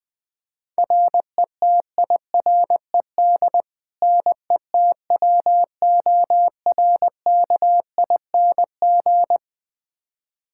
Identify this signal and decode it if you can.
Morse code, heard as a keyed tone, about 21 wpm, 700 Hz. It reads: RETIRED NETWORKING
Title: K1USN SST QSO Practice 20wpm